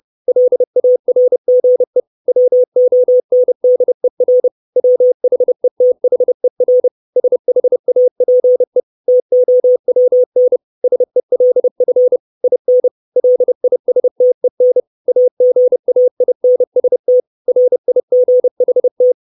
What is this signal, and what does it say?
Morse code, heard as a keyed tone, about 30 wpm, 500 Hz. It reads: LARGE WONDER WHETHER SHAPE TOWN SELF IN LISTEN AGAINST RIGHT
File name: 632 Words 30WPM